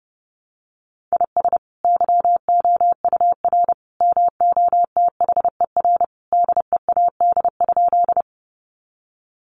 Morse code, heard as a keyed tone, about 30 wpm, 700 Hz. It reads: IS YOUR MOTHER DEAD?
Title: Sentences from Top 1500 Words 30wpm